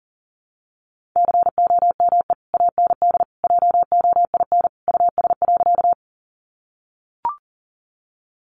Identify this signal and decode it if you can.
Morse code, heard as a keyed tone, about 40 wpm, 700 Hz. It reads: COME AND JOIN US.
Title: Sentences from Top 1400 Words 40wpm